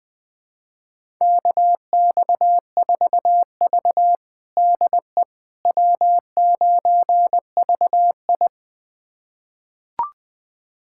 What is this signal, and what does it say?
Morse code, heard as a keyed tone, about 21 wpm, 700 Hz. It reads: KX4V DE W9VI
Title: USA Callsign DE Callsign - 4 Characters - 20wpm